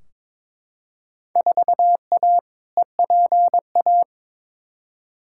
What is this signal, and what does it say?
Morse code, heard as a keyed tone, about 22 wpm, 700 Hz. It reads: 4A EPA
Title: Field Day Exchanges - 22wpm